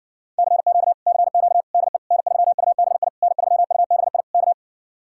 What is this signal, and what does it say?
Morse code, heard as a keyed tone, about 60 wpm, 700 Hz. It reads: CQ CQ DE N3UBI N3UBI K